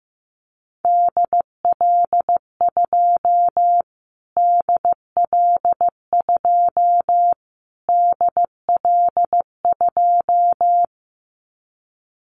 Morse code, heard as a keyed tone, about 15 wpm, 700 Hz. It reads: DL2 DL2 DL2
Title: Callsign Prefixes for Top 25 DXCC - Repeated 3x - 15wpm